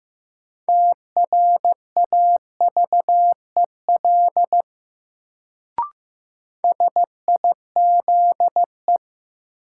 Morse code, heard as a keyed tone, about 15 wpm, 700 Hz. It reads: TRAVEL SIZE
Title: Sentences from Top 400 Words - New Word Drill - 15wpm